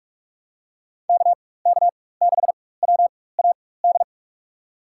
Morse code, heard as a keyed tone, about 46 wpm, spaced as for 21 wpm, 700 Hz. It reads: KK6WAD
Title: USA Call Signs - 6 Characters - 4x Character Spacing - 45wpm